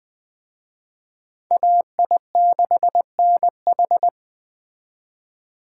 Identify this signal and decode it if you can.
Morse code, heard as a keyed tone, about 23 wpm, 700 Hz. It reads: AI6NH
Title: Speed Racing - USA 2x2 Call Signs 20wpm